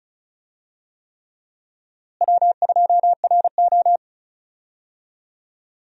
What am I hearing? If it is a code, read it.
Morse code, heard as a keyed tone, about 36 wpm, 700 Hz. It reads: W2RO
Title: USA Call Signs - 4 Characters - Repeated 3x - 35wpm